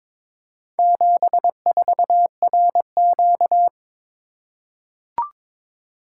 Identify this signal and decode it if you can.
Morse code, heard as a keyed tone, about 23 wpm, 700 Hz. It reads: 74RQ
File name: practice / 22wpm / Four Letter-Number Combinations - 22wpm